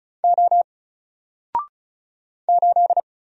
Morse code, heard as a keyed tone, about 36 wpm, 700 Hz. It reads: O 8
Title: Single Character (Letter:Number:Punctuation:Prosign) 35wpm